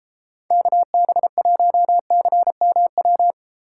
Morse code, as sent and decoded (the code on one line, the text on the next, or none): -.- -... .---- -.-. -- .--
KB1CMW